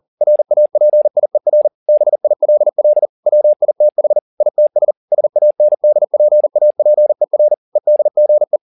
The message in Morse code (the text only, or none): RAPIER BILL WITH ITS SANDPAPER EDGE